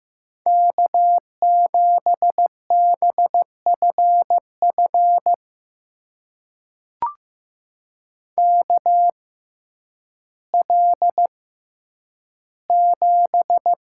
Morse code, heard as a keyed tone, about 15 wpm, spaced as for 9 wpm, 700 Hz. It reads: K7BFF KL7
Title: USA Call Signs - 5 Characters - 6x Character Spacing - 15wpm